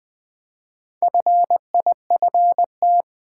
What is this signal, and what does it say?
Morse code, heard as a keyed tone, about 20 wpm, 700 Hz. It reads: FIFT
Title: Build A Word - 1 Letter At A Time - 5-Letter Words - 20wpm